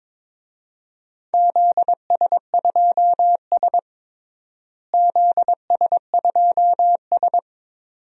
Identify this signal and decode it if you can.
Morse code, heard as a keyed tone, about 23 wpm, 700 Hz. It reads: ZS2S ZS2S
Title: DXCC-100 Call Signs - 4 Characters - Repeated 2x - 22wpm